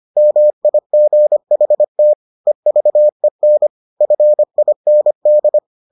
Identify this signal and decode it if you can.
Morse code, heard as a keyed tone, about 25 wpm, 600 Hz. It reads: MIGHT EVEN FIND